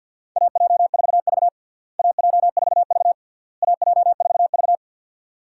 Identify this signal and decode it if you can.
Morse code, heard as a keyed tone, about 51 wpm, 700 Hz. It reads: AJ4V AJ4V AJ4V
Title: USA Call Signs - 4 Characters - Repeated 3x - 50wpm